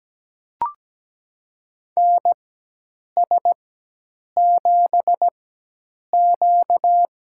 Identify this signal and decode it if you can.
Morse code, heard as a keyed tone, about 17 wpm, spaced as for 8 wpm, 700 Hz. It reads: NS7Q
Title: USA Call Signs - 4 Characters - 4x Character Spacing - 17wpm